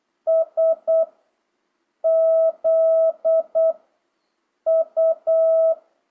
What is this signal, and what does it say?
Morse code, heard as a keyed tone, about 8 wpm, 660 Hz. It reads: S Z U